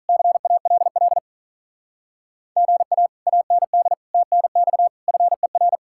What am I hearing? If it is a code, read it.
Morse code, heard as a keyed tone, about 45 wpm, 700 Hz. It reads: CALL GA AND TNX FER